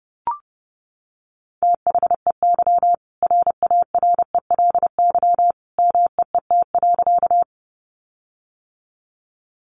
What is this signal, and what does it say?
Morse code, heard as a keyed tone, about 28 wpm, 700 Hz. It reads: THEY RARELY MEET.
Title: Sentences from Top 1400 Words 30wpm